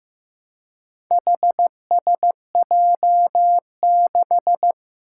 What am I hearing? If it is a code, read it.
Morse code, heard as a keyed tone, about 15 wpm, 700 Hz. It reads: HSJ6
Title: Four Challenging Letter-Number Combinations 15wpm